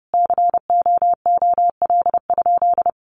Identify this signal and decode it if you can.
Morse code, heard as a keyed tone, about 30 wpm, 700 Hz. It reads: COOL?
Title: Sentences from Top 1200 Words 30wpm